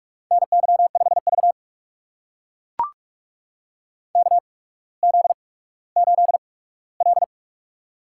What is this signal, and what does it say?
Morse code, heard as a keyed tone, about 44 wpm, spaced as for 19 wpm, 700 Hz. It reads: NY5V KZ8L
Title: USA Call Signs - 4 Characters - 8x Character Spacing - 45wpm